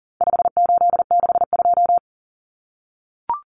Morse code, heard as a keyed tone, about 40 wpm, 700 Hz. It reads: 5862
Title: Common Cut Numbers - 4-Digit 40wpm